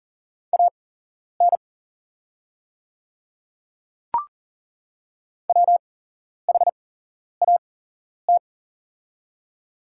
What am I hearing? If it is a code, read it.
Morse code, heard as a keyed tone, about 40 wpm, spaced as for 11 wpm, 690 Hz. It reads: AN WHAT
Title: 8x Character Spacing - Top 200 Words - 40wpm